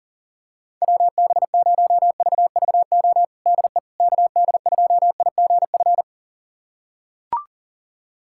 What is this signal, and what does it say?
Morse code, heard as a keyed tone, about 37 wpm, 700 Hz. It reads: WB0VVO DE KD2IGF
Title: USA Callsign DE Callsign - 6 Characters - 40wpm